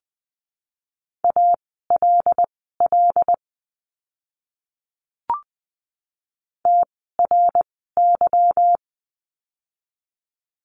Morse code, heard as a keyed tone, about 20 wpm, spaced as for 15 wpm, 700 Hz. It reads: ALL TRY
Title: 160 Lesson 26f - Letter g - Words 20wpm